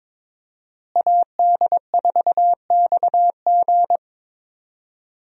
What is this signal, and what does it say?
Morse code, heard as a keyed tone, about 22 wpm, 700 Hz. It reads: AD4XG
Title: USA Call Signs - 5 Characters - Repeated 2x - 22wpm